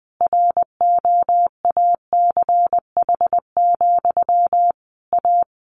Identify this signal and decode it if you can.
Morse code, heard as a keyed tone, about 20 wpm, 700 Hz. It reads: ROACH, A
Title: A Sound Of Thunder 20wpm